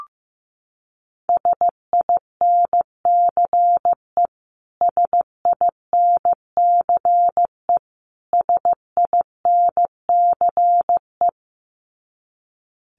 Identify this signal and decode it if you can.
Morse code, heard as a keyed tone, about 15 wpm, 700 Hz. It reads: SINCE SINCE SINCE
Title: Top 300 Words Repeated 15wpm